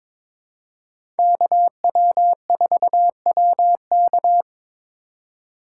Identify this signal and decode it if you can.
Morse code, heard as a keyed tone, about 23 wpm, 700 Hz. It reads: KW4WK
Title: USA Call Signs - 5 Characters - 3x Character Spacing - 22wpm